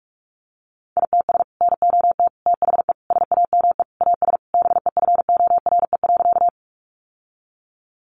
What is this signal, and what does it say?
Morse code, heard as a keyed tone, about 44 wpm, 700 Hz. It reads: ITS NOT THE SAME AS BEFORE.
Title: Sentences from Top 600 Words 45wpm